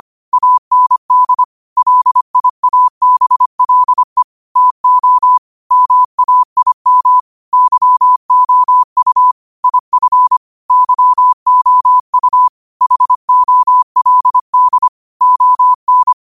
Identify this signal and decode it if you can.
Morse code, heard as a keyed tone, about 25 wpm, 1000 Hz. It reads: AND LIABLE TO MAIM YOU IF YOU HOLD ON